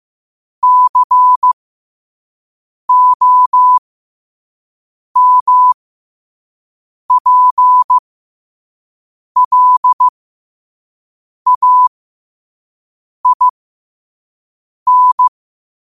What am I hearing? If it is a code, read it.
Morse code, heard as a keyed tone, about 15 wpm, spaced as for 5 wpm, 1000 Hz. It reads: COMPLAIN
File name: qotdfiles / 5wpm